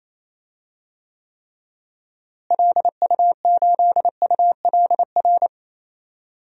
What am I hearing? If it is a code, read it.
Morse code, heard as a keyed tone, about 28 wpm, 700 Hz. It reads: LU8ULR
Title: DXCC-100 Call Signs - 6 Characters - Repeated 3x - 28wpm